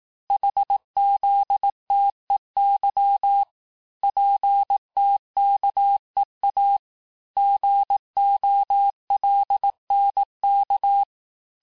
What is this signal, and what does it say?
Morse code, heard as a keyed tone, about 18 wpm, 800 Hz. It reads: HZTEY PTKEA GOLNK
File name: cours-du-lundi files / cours-1852-texte-chiffre-vitesse-17MPM